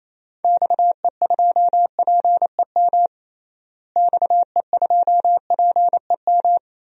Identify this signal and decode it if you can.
Morse code, heard as a keyed tone, about 28 wpm, 700 Hz. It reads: XE2PEM XE2PEM
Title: DXCC-100 Call Signs - 6 Characters - Repeated 3x - 28wpm